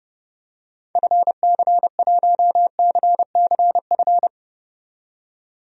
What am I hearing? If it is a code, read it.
Morse code, heard as a keyed tone, about 31 wpm, 700 Hz. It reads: FC1CCF
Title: DXCC-100 Call Signs - 6 Characters - 30wpm